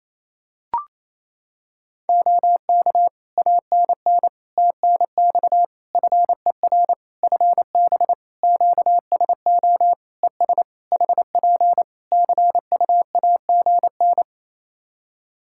Morse code, heard as a keyed tone, about 27 wpm, 700 Hz. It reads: OK ANN TNX FER FB QSO ES HP CUAGN
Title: K1USN SST QSO Practice 28wpm